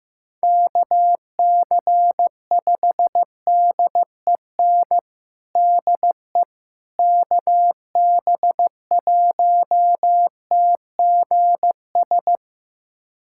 Morse code, heard as a keyed tone, about 15 wpm, 700 Hz. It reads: KC5DEN DE KB1TGS
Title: USA Callsign DE Callsign - 6 Characters - 15wpm